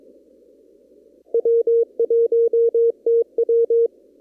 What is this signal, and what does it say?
Morse code, heard as a keyed tone, about 22 wpm, 460 Hz. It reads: W1TW
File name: CW / kt5j wpx cw-op-k5pi